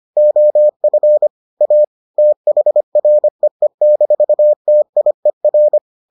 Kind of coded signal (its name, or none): Morse code